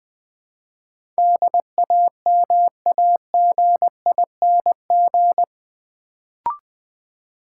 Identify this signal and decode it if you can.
Morse code, heard as a keyed tone, about 20 wpm, 700 Hz. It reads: DAMAGING
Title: Build A Word - 2 Letters At A Time - 8-Letter Words - 20wpm